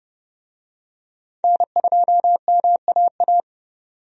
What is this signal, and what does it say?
Morse code, heard as a keyed tone, about 30 wpm, 700 Hz. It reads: N2MAA
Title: USA Call Signs - 5 Characters - 30wpm